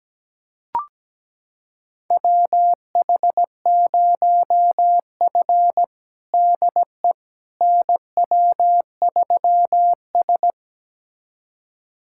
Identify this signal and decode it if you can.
Morse code, heard as a keyed tone, about 17 wpm, 700 Hz. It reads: WH0F DE NW3S
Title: USA Callsign DE Callsign - 4 Characters - 17wpm